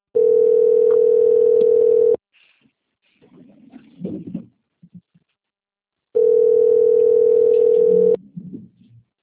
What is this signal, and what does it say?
Ringback tone (North America), 440 + 480 Hz.